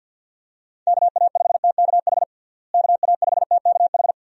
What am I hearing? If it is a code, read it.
Morse code, heard as a keyed tone, about 48 wpm, 700 Hz. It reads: KA5TKH KA5TKH
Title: USA Call Signs - 6 Characters - Repeated 2x - 50wpm